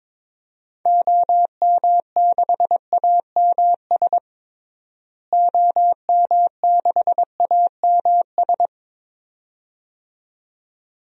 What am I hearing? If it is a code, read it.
Morse code, heard as a keyed tone, about 22 wpm, 700 Hz. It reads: OM6AMS OM6AMS
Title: DXCC-100 Call Signs - 6 Characters - Repeated 2x - 22wpm